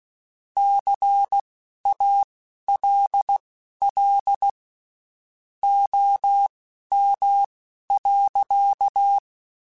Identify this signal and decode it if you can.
Morse code, heard as a keyed tone, about 16 wpm, spaced as for 11 wpm, 800 Hz. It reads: CALL OM.
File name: (spaced) / QSOs Spaced-mp3s / kc7haa @15 sp